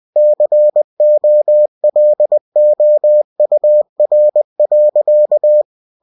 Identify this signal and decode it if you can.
Morse code, heard as a keyed tone, about 20 wpm, 600 Hz. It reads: COLOUR.
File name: cw-books / winnie-the-pooh / winnie-20wpm-page-0037